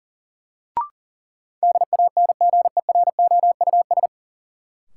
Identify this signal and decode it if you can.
Morse code, heard as a keyed tone, about 40 wpm, 700 Hz. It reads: DANGEROUS
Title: Sentences from Top 1200 Words - New Word Drill Alt - 40wpm